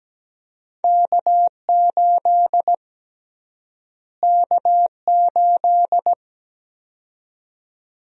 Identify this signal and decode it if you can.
Morse code, heard as a keyed tone, about 17 wpm, 700 Hz. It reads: K8 K8
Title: USA Call Sign Prefixes - Repeated 2x - 17wpm